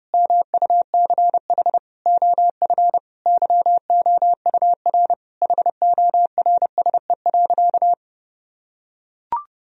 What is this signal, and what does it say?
Morse code, heard as a keyed tone, about 30 wpm, 700 Hz. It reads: MUCH OF YOUR HORSE.
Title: Sentences from Top 1000 Words - Review Part 2 - 30wpm